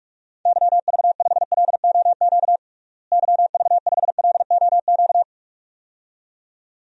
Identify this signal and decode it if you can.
Morse code, heard as a keyed tone, about 46 wpm, 700 Hz. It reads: YV5LOQ YV5LOQ
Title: DXCC-100 Call Signs - 6 Characters - Repeated 2x - 45wpm